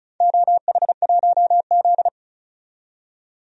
Morse code, heard as a keyed tone, about 36 wpm, 700 Hz. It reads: OH1Z
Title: DXCC-100 Call Signs - 4 Characters - 35wpm